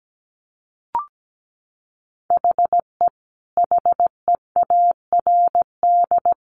Morse code, heard as a keyed tone, about 17 wpm, 700 Hz. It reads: HE HEARD
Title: Sentences from Top 1100 Words 17wpm